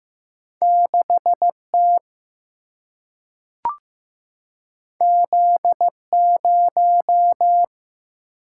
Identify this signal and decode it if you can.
Morse code, heard as a keyed tone, about 16 wpm, 700 Hz. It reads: 6T Z0
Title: Single-Letter Single-Number 15wpm